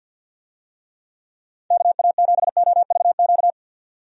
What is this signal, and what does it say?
Morse code, heard as a keyed tone, about 51 wpm, 700 Hz. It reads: KA7OVQ
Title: USA Call Signs - 6 Characters - 50wpm